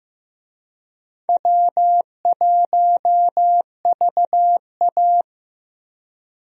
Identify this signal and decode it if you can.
Morse code, heard as a keyed tone, about 15 wpm, 700 Hz. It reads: W1VA
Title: USA Call Signs - 4 Characters - Repeated 3x - 15wpm